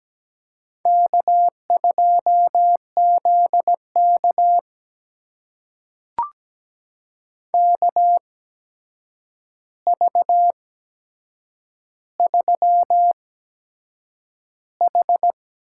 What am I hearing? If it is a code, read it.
Morse code, heard as a keyed tone, about 17 wpm, spaced as for 7 wpm, 700 Hz. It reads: K2ZK KV3H
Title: USA Call Signs - 4 Characters - 8x Character Spacing - 17wpm